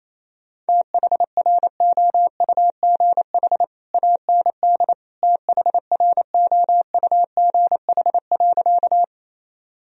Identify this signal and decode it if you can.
Morse code, heard as a keyed tone, about 28 wpm, 700 Hz. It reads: THROUGH AND THROUGH.
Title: Sentences from Top 1000 Words - Review Part 1 - 28wpm